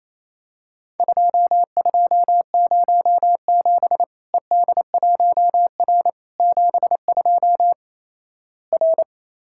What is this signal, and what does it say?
Morse code, heard as a keyed tone, about 28 wpm, 700 Hz. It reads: 2207 ED1R 72 R
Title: Worked All Europe DX Contest - QTC Exchange - 28wpm